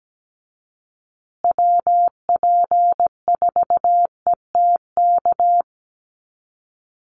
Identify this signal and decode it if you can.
Morse code, heard as a keyed tone, about 17 wpm, 700 Hz. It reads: WP4ETK
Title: USA Call Signs Repeated 2x 17wpm